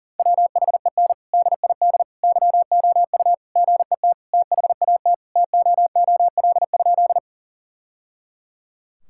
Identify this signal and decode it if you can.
Morse code, heard as a keyed tone, about 40 wpm, 700 Hz. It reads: WHEN DID YOU GET THAT TOOL?
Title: Sentences from Top 1000 Words - Review Part 6 - 40wpm